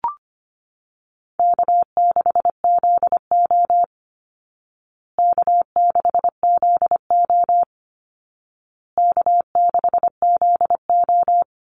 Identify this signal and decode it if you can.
Morse code, heard as a keyed tone, about 25 wpm, 700 Hz. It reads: K6ZO K6ZO K6ZO
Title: Easy Call Signs Repeated 3x - 25wpm